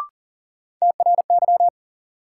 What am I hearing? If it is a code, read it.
Morse code, heard as a keyed tone, about 38 wpm, 700 Hz. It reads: TRY
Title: Sets of 2 Words - New Word Drill - 40wpm